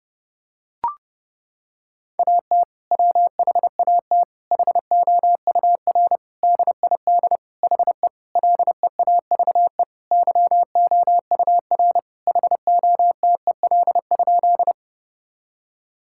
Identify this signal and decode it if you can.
Morse code, heard as a keyed tone, about 30 wpm, 700 Hz. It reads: AT WHAT HOUR DID HE LEAVE YOUR HOTEL?
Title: Sentences from Top 1600 Words 30wpm